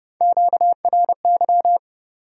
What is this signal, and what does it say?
Morse code, heard as a keyed tone, about 26 wpm, 700 Hz. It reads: QRY